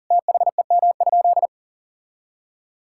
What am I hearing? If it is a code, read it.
Morse code, heard as a keyed tone, about 40 wpm, 700 Hz. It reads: THEM?
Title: Sentences from Top 1000 Words - Review Part 6 - 40wpm